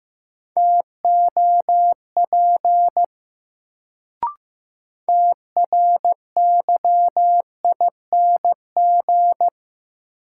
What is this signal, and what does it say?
Morse code, heard as a keyed tone, about 15 wpm, 700 Hz. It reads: TOP TRYING
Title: Sets of 2 Words - New Word Drill - 15wpm